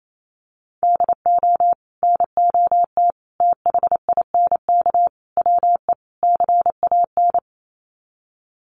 Morse code, heard as a keyed tone, about 28 wpm, 700 Hz. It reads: DO NOT THINK WE CAN
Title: Sets of 5 Words Encore2 28wpm